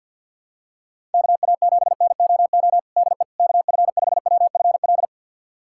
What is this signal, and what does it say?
Morse code, heard as a keyed tone, about 52 wpm, 700 Hz. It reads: KA7NOO DE KF5WFL
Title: USA Callsign DE Callsign 50wpm